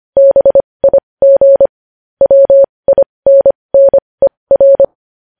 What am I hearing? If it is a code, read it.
Morse code, heard as a keyed tone, about 25 wpm, spaced as for 20 wpm, 550 Hz. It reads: BIG WINNER
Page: News Headlines in Morse Code at 20 WPM